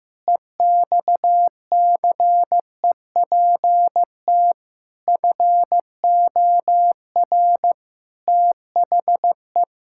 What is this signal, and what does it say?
Morse code, heard as a keyed tone, about 16 wpm, 700 Hz. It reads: EXCEPT FOR THE
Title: Pangram Challenge v2 - 15wpm